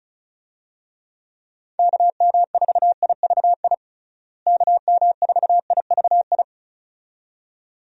Morse code, heard as a keyed tone, about 36 wpm, 700 Hz. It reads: KM4IVI KM4IVI
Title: USA Call Signs - 6 Characters - Repeated 2x - 35wpm